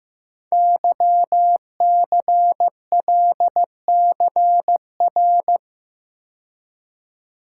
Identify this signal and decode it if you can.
Morse code, heard as a keyed tone, about 15 wpm, 700 Hz. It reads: YCLCR
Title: Code Groups - 5 Letters - 15wpm